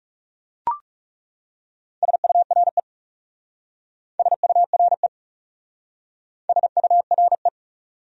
Morse code, heard as a keyed tone, about 39 wpm, 700 Hz. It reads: SURE SURE SURE